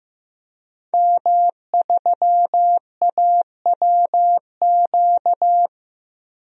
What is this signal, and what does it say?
Morse code, heard as a keyed tone, about 16 wpm, 700 Hz. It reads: M3AWQ
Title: DXCC-100 Call Signs - 5 Characters - 15wpm